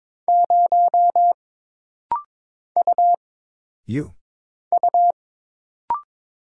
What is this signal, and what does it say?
Morse code, heard as a keyed tone, about 22 wpm, 700 Hz. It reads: 0 U U
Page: This rapid-fire timing is geared to push your ICR (Instant Character Recognition) skills.